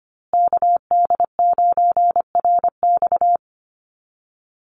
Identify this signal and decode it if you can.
Morse code, heard as a keyed tone, about 25 wpm, 700 Hz. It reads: KD9RX
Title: USA Call Signs Repeated 3x 25wpm